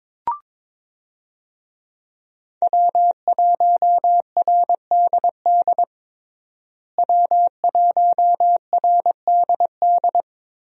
Morse code, heard as a keyed tone, about 22 wpm, 700 Hz. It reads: W1RDD W1RDD
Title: USA Call Signs - 5 Characters - Repeated 3x - 22wpm